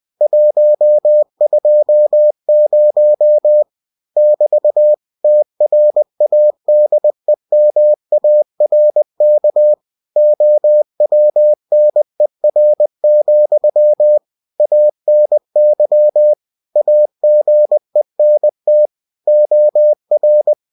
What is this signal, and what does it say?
Morse code, heard as a keyed tone, about 20 wpm, 600 Hz. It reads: 120 <BT> TRADEMARK OWNER, ANY AGENT OR
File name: cw-books / winnie-the-pooh / winnie-20wpm-page-0120